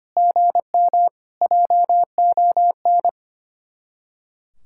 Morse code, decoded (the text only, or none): GM JON